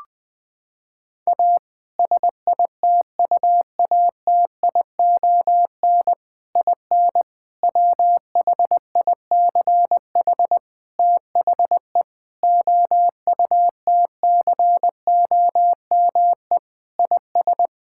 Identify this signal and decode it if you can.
Morse code, heard as a keyed tone, about 21 wpm, 700 Hz. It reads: A SITUATION IN WHICH THE OUTCOME IS
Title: Sentences with 3-Word Phrases - 20wpm